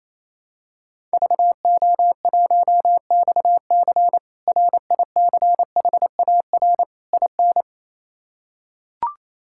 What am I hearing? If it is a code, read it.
Morse code, heard as a keyed tone, about 28 wpm, 700 Hz. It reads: VO1XC RICHAR IN
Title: K1USN SST QSO Practice - 28wpm